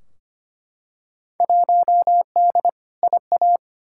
Morse code, heard as a keyed tone, about 25 wpm, 700 Hz. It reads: 1D IA
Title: Field Day Exchanges - 25wpm